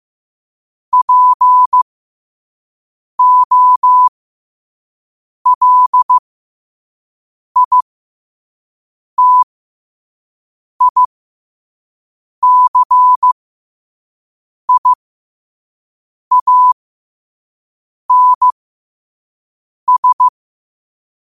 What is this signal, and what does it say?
Morse code, heard as a keyed tone, about 15 wpm, spaced as for 5 wpm, 1000 Hz. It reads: POLITICIANS
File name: qotdfiles / 5wpm